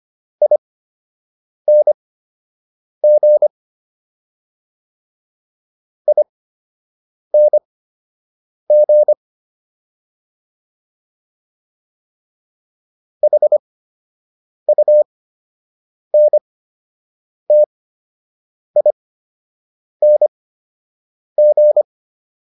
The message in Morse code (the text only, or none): ING ING HUNTING